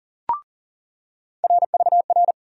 Morse code, heard as a keyed tone, about 40 wpm, 700 Hz. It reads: RVR
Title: USA Call Sign Postfixes - 3 Letters 40wpm